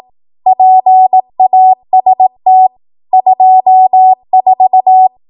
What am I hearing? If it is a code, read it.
Morse code, heard as a keyed tone, about 18 wpm, 750 Hz. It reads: PAST 24
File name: Morse / 18WPM